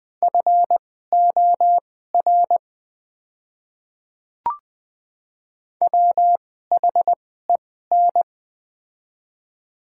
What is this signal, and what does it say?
Morse code, heard as a keyed tone, about 21 wpm, spaced as for 15 wpm, 700 Hz. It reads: FOR WHEN
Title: 2x Character Spacing - Top 100 Words - 20wpm